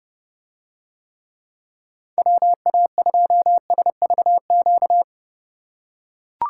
Morse code, heard as a keyed tone, about 28 wpm, 700 Hz. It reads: WA2SVQ